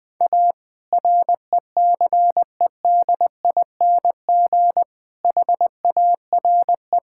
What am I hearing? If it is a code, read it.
Morse code, heard as a keyed tone, about 20 wpm, 700 Hz. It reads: A RECEDING HARE
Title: All-Age Appropriate Jokes - 20wpm